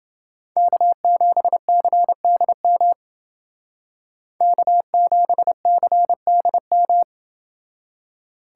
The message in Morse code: K7CDM K7CDM

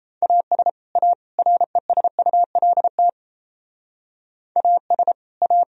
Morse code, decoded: AS A RESULT AS A

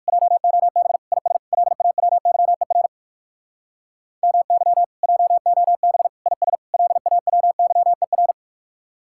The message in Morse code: JOB IS LAWYER MY JOB IS LAWYER

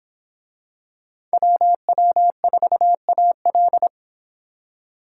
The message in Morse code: WW4AL

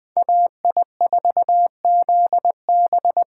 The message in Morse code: AI4ZB

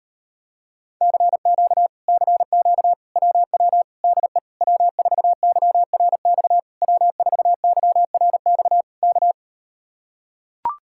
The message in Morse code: CQ CQ WW DE W4YRX W4YRX K

